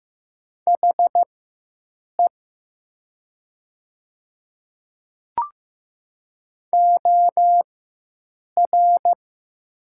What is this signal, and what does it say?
Morse code, heard as a keyed tone, about 15 wpm, spaced as for 7 wpm, 700 Hz. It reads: HE OR
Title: 4x Character Spacing - Top 100 Words - 15wpm